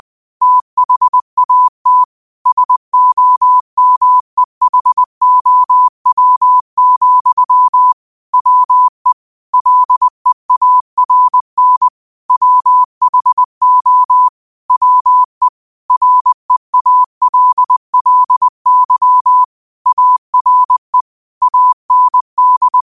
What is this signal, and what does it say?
Morse code, heard as a keyed tone, about 20 wpm, 1000 Hz. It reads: THAT SOMEHOW, WE LEARN WHO WE REALLY ARE AND
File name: qotdfiles / 20wpm